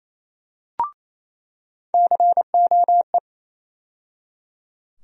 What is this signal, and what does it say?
Morse code, heard as a keyed tone, about 28 wpm, 700 Hz. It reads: COE
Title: USA Call Sign Postfixes - 3 Letters 28wpm